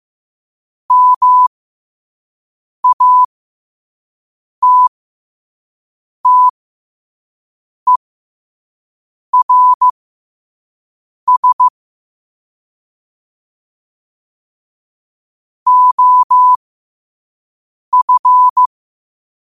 Morse code, heard as a keyed tone, about 15 wpm, spaced as for 5 wpm, 1000 Hz. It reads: MATTERS OF
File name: qotdfiles / 5wpm